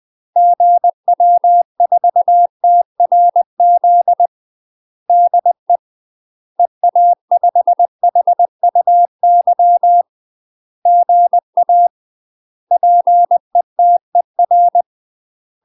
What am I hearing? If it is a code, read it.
Morse code, heard as a keyed tone, about 20 wpm, 700 Hz. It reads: GW4TRZ DE EA5HUY GA PETER
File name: UK MASTER QSOs TRAINING/20WPM